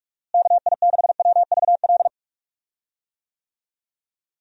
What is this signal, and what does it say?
Morse code, heard as a keyed tone, about 44 wpm, 700 Hz. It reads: KI6WVL
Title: USA Call Signs - 6 Characters - Repeated 3x - 45wpm